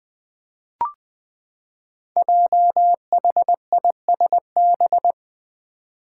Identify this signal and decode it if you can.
Morse code, heard as a keyed tone, about 20 wpm, 700 Hz. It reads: JHISB
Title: Five Challenging Letter-Number Combinations 20wpm